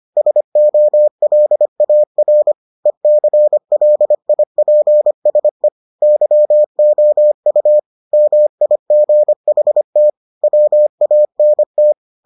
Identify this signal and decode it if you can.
Morse code, heard as a keyed tone, about 25 wpm, 600 Hz. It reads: SOLAR ECLIPSE YOU MIGHT WANT